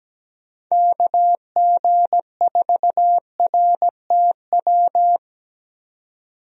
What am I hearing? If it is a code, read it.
Morse code, heard as a keyed tone, about 17 wpm, 700 Hz. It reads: KG4RTW
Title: USA Call Signs - 6 Characters - 17wpm